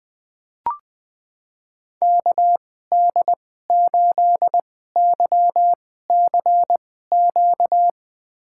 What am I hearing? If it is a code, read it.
Morse code, heard as a keyed tone, about 20 wpm, spaced as for 14 wpm, 700 Hz. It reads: KD8YCQ
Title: USA Call Signs - 2x Character Spacing - 20wpm